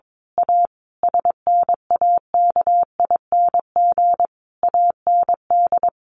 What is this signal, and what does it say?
Morse code, heard as a keyed tone, about 22 wpm, 700 Hz. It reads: A SNAKING AND
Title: A Sound Of Thunder 22wpm